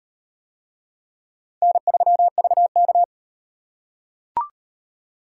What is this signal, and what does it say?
Morse code, heard as a keyed tone, about 38 wpm, 700 Hz. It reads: N3VK